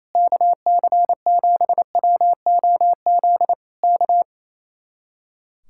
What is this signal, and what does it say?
Morse code, heard as a keyed tone, about 27 wpm, 700 Hz. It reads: KC7WOZ K
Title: Speed-Racing Common QSO Phrases - 28wpm